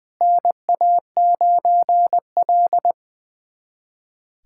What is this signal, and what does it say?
Morse code, heard as a keyed tone, about 20 wpm, 700 Hz. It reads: NA9L
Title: USA Callsign DE Callsign - 4 Characters - 20wpm